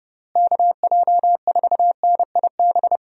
Morse code, heard as a keyed tone, about 32 wpm, 700 Hz. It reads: KJ4NIB
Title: USA Call Signs - 6 Characters - Repeated 3x - 30wpm